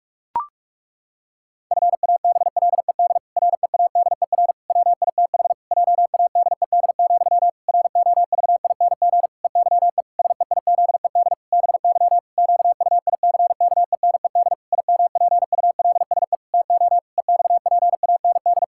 Morse code, heard as a keyed tone, about 41 wpm, 700 Hz. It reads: FABLED READER WITH JADED, ROVING EYE SEIZED BY QUICKENED IMPULSE TO EXPAND
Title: Pangram Challenge v2 - 45wpm